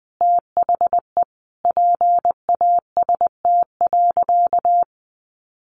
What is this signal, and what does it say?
Morse code, heard as a keyed tone, about 20 wpm, 700 Hz. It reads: THE PAST.
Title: A Sound Of Thunder 20wpm